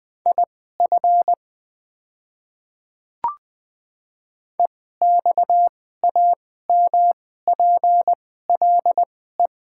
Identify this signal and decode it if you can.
Morse code, heard as a keyed tone, about 20 wpm, spaced as for 14 wpm, 700 Hz. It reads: IF EXAMPLE
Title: 2x Character Spacing - Top 500 Words - 20wpm